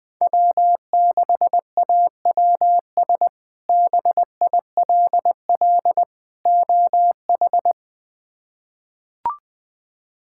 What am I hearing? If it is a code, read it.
Morse code, heard as a keyed tone, about 20 wpm, 700 Hz. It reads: W6AWS BILL OH
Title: K1USN SST QSO Practice - 20wpm